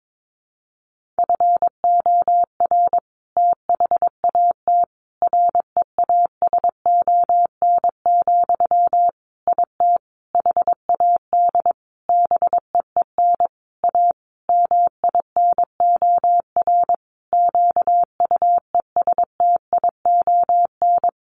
Morse code, heard as a keyed tone, about 22 wpm, 700 Hz. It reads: FOR THAT REASON, IT HAD BEEN A MINOR QUESTION
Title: Sentences from Top 1500 Words 22wpm